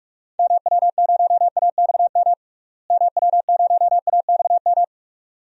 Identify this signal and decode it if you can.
Morse code, heard as a keyed tone, about 46 wpm, 700 Hz. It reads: MW0AXM MW0AXM
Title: DXCC-100 Call Signs - 6 Characters - Repeated 2x - 45wpm